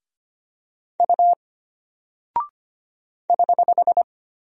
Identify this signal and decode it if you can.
Morse code, heard as a keyed tone, about 25 wpm, 700 Hz. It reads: U <HH>
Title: Single Character (Letter:Number:Punctuation:Prosign) Alt 25wpm